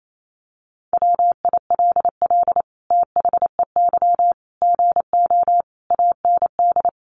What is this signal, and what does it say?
Morse code, heard as a keyed tone, about 28 wpm, 700 Hz. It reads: WILL THEY GO AND
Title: Sentences from Top 1100 Words 28wpm